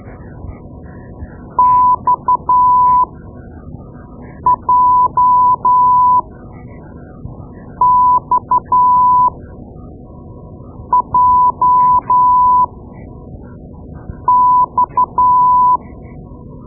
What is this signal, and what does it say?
Morse code, heard as a keyed tone, about 12 wpm, spaced as for 5 wpm, 1000 Hz. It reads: XJXJX
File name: Morse Lessons / Morse Lessons / Tape-2B